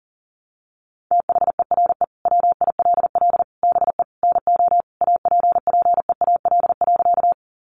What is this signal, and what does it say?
Morse code, heard as a keyed tone, about 37 wpm, 700 Hz. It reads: THERE WILL BE NO APPEAL.
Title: Sentences from Top 1300 Words 40wpm